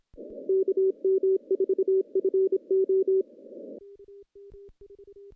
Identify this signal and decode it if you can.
Morse code, heard as a keyed tone, about 26 wpm, 380 Hz. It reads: KM4FO
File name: contests / CW-SS-k5tr- -1400